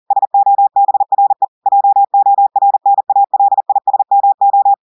Morse code, heard as a keyed tone, about 40 wpm, 800 Hz. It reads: SOBRE JORNALISMO
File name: audios / portugais / portugais texte en clair 1 vitesse 40 mots minutes